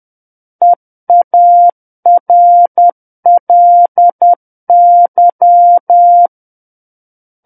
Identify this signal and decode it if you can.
Morse code, heard as a keyed tone, about 10 wpm, 700 Hz. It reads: EARLY
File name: Book One / 01-Chapter 1 - 10 WPM